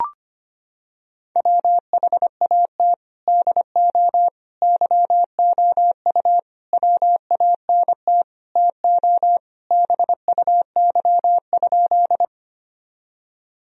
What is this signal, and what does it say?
Morse code, heard as a keyed tone, about 25 wpm, 700 Hz. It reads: WHAT DO YOU WANT TO BUY?
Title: Sentences from Top 1000 Words - Review Part 4 - 25wpm